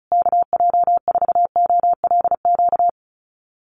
Morse code, heard as a keyed tone, about 34 wpm, 700 Hz. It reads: KJ4OLQ
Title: USA Call Signs Repeated 2x 35wpm